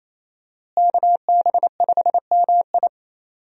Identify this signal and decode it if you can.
Morse code, heard as a keyed tone, about 27 wpm, 700 Hz. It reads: KB5MI
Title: USA Call Signs - 5 Characters - Repeated 3x - 28wpm